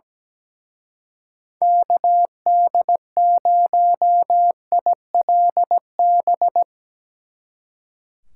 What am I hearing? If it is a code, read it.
Morse code, heard as a keyed tone, about 17 wpm, 700 Hz. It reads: KD0ILB
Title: Speed Racing - USA 2x3 Call Signs 17wpm